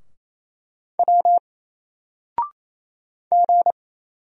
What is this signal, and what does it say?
Morse code, heard as a keyed tone, about 27 wpm, 700 Hz. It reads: W G
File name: Single-Letter-Number-28wpm